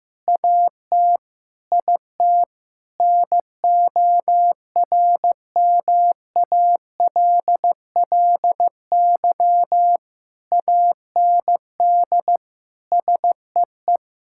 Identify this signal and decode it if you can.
Morse code, heard as a keyed tone, about 15 wpm, 700 Hz. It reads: AT IT NORMALLY AND SEE
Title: Sentences from Top 1000 Words - Review Part 6 - 15wpm